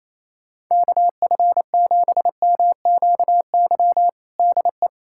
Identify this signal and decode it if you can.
Morse code, heard as a keyed tone, about 28 wpm, 700 Hz. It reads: KF7MQY DE
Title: USA Callsign DE Callsign - 6 Characters - 28wpm